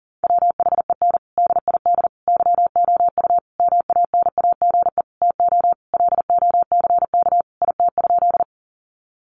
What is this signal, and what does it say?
Morse code, heard as a keyed tone, about 40 wpm, 700 Hz. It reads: WHEN DID YOU MANAGE TO LOCK IT?
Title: Sentences from Top 1400 Words 40wpm